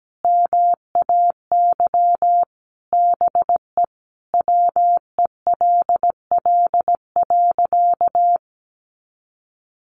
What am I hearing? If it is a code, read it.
Morse code, heard as a keyed tone, about 17 wpm, 700 Hz. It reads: MAY BE WELL.
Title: Sentences from Top 1200 Words 17wpm